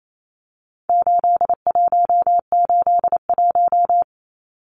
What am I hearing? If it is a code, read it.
Morse code, heard as a keyed tone, about 28 wpm, 700 Hz. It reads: 8181
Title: Numbers - 4-Digit 28wpm